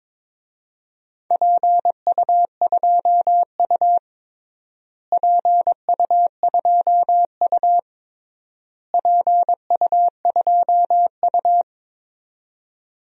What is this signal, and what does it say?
Morse code, heard as a keyed tone, about 22 wpm, 700 Hz. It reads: PU2U PU2U PU2U
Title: DXCC-100 Call Signs - 4 Characters - Repeated 3x - 22wpm